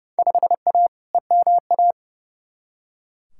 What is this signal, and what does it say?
Morse code, heard as a keyed tone, about 30 wpm, 700 Hz. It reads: 5A EMA
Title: Field Day Exchanges Encore - 30wpm